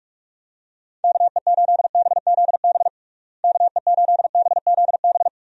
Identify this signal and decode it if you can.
Morse code, heard as a keyed tone, about 46 wpm, 700 Hz. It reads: KE8BZB KE8BZB
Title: USA Call Signs - 6 Characters - Repeated 3x - 45wpm